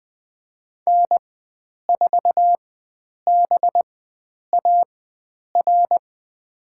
Morse code, heard as a keyed tone, about 20 wpm, spaced as for 9 wpm, 700 Hz. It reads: N4BAR
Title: USA Call Signs - 5 Characters - 4x Character Spacing - 20wpm